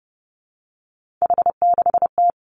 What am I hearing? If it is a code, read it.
Morse code, heard as a keyed tone, about 30 wpm, 700 Hz. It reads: H6T
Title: Three Challenging Letter-Number Combinations 30wpm